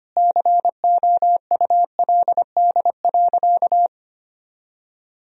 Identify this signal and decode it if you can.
Morse code, heard as a keyed tone, about 25 wpm, 700 Hz. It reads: COULD.
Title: Sentences from Top 1000 Words - Review Part 1 - 25wpm